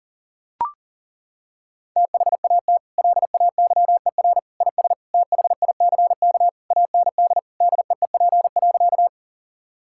Morse code, heard as a keyed tone, about 37 wpm, 700 Hz. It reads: THAT LAYER IS THICK AND DEEP.
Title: Sentences from Top 1600 Words 40wpm